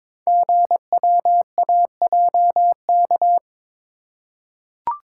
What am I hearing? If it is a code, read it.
Morse code, heard as a keyed tone, about 22 wpm, 700 Hz. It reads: GWAJK
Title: Code Groups - 5 Letter:Numbers - 22wpm